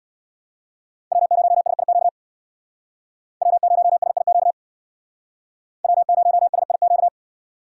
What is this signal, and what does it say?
Morse code, heard as a keyed tone, about 63 wpm, 700 Hz. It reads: W0SIQ W0SIQ W0SIQ